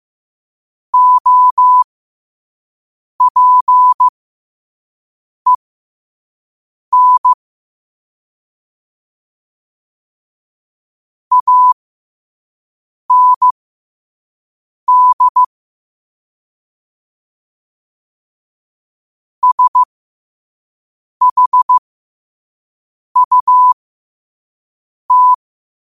Morse code, heard as a keyed tone, about 15 wpm, spaced as for 5 wpm, 1000 Hz. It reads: OPEN AND SHUT
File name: qotdfiles / 5wpm